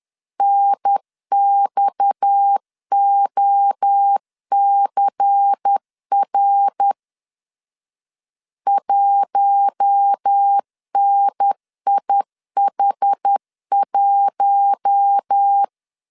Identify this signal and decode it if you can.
Morse code, heard as a keyed tone, about 10 wpm, 800 Hz. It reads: NXOCR 1NIH1
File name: LEZIONI / 6 PKX0 / 6 PKX 10wpm 2426